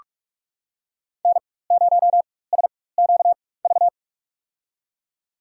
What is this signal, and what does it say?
Morse code, heard as a keyed tone, about 46 wpm, spaced as for 21 wpm, 700 Hz. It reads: N0SQV
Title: USA Call Signs - 5 Characters - 4x Character Spacing - 45wpm